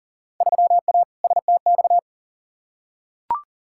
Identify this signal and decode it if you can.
Morse code, heard as a keyed tone, about 40 wpm, 700 Hz. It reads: 3A STX
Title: Field Day Exchanges - 40wpm